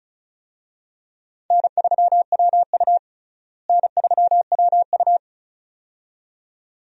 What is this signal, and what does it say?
Morse code, heard as a keyed tone, about 36 wpm, 700 Hz. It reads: N3WU N3WU
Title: USA Call Signs - 4 Characters - Repeated 2x - 35wpm